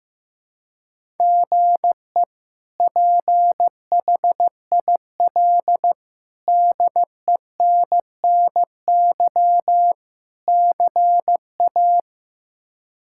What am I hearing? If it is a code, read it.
Morse code, heard as a keyed tone, about 15 wpm, 700 Hz. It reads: GE PHIL DENNY CA
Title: K1USN SST QSO Practice - 15wpm